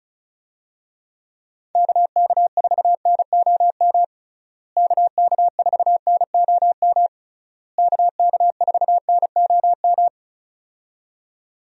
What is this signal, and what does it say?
Morse code, heard as a keyed tone, about 34 wpm, 700 Hz. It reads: KK4NOM KK4NOM KK4NOM
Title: USA Call Signs - 6 Characters - Repeated 3x - 35wpm